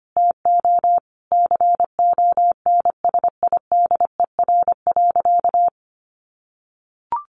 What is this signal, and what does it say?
Morse code, heard as a keyed tone, about 25 wpm, 700 Hz. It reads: TO CONSIDER.
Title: Sentences from Top 900 Words 25wpm